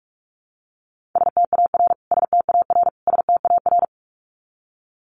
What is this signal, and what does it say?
Morse code, heard as a keyed tone, about 44 wpm, 700 Hz. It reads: STAR STAR STAR
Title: Top 500 Words Repeated 45wpm